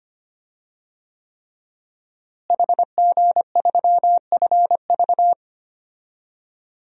Morse code, heard as a keyed tone, about 25 wpm, 700 Hz. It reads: HG3FV
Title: DXCC-100 Call Signs - 5 Characters - Repeated 3x - 25wpm